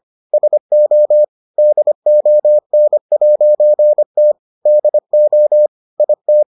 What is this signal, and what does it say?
Morse code, heard as a keyed tone, about 25 wpm, 600 Hz. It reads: SO DON'T DO IT